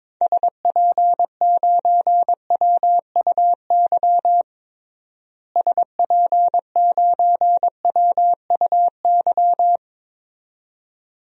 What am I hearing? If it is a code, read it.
Morse code, heard as a keyed tone, about 22 wpm, 700 Hz. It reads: SP9WUY SP9WUY
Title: DXCC-100 Call Signs - 6 Characters - Repeated 2x - 22wpm